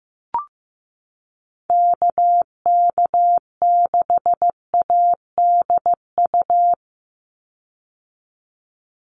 Morse code, heard as a keyed tone, about 15 wpm, 700 Hz. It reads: KK6ADU
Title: USA Call Signs 15wpm